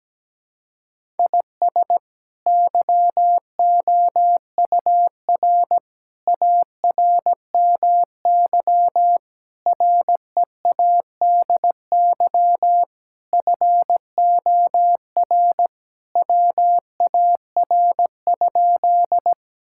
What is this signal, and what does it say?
Morse code, heard as a keyed tone, about 17 wpm, 700 Hz. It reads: IS YOUR ARMY READY FOR WAR?
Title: Sentences from Top 1000 Words - Review Part 5 - 17wpm